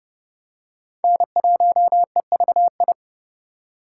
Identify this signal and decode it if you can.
Morse code, heard as a keyed tone, about 30 wpm, 700 Hz. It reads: N1EVI
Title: USA Call Signs - 5 Characters - 8x Character Spacing - 30wpm